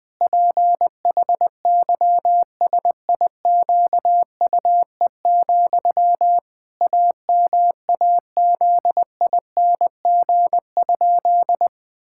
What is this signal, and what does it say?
Morse code, heard as a keyed tone, about 20 wpm, 700 Hz. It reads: PHYSIQUE, AMAZING?
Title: Pangram Challenge v2 - 20wpm